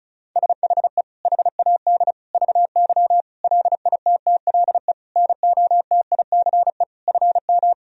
Morse code, heard as a keyed tone, about 36 wpm, 700 Hz. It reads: SHE HAD VY LITTLE NOTICE FM
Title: Sentences from Top 400 Words and Abbreviations 35wpm